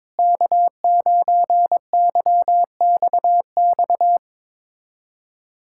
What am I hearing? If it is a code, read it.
Morse code, heard as a keyed tone, about 21 wpm, 700 Hz. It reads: K9YXX
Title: USA Callsign DE Callsign - 5 Characters - 22wpm